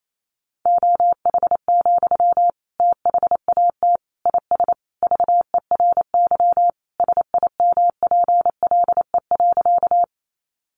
Morse code, heard as a keyed tone, about 28 wpm, 700 Hz. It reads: OH, THAT IS VERY SIMPLE.
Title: Sentences from Top 500 Words 28wpm